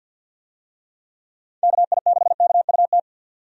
Morse code, heard as a keyed tone, about 51 wpm, 700 Hz. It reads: KI6KUT
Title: USA Call Signs - 6 Characters - 50wpm